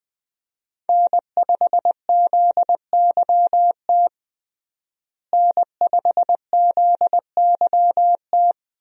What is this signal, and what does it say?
Morse code, heard as a keyed tone, about 21 wpm, 700 Hz. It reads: N5ZYT N5ZYT
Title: USA Call Signs - 5 Characters - Repeated 2x - 20wpm